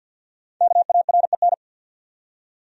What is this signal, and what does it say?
Morse code, heard as a keyed tone, about 52 wpm, 700 Hz. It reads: KAREN
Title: USA Common Names - Max 5 Letters - 50wpm